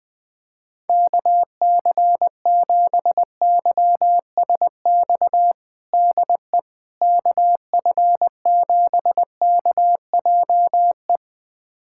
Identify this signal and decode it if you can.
Morse code, heard as a keyed tone, about 20 wpm, 700 Hz. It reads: KC7YSX DE KF7KJE
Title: USA Callsign DE Callsign - 6 Characters - 20wpm